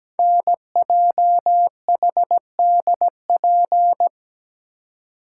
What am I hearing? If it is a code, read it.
Morse code, heard as a keyed tone, about 17 wpm, 700 Hz. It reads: NJHDP
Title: Code Groups - 5 Letters - 17wpm